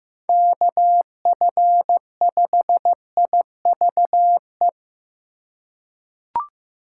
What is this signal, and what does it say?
Morse code, heard as a keyed tone, about 15 wpm, 700 Hz. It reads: KF5IVE
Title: USA Call Signs - 6 Characters - 4x Character Spacing - 15wpm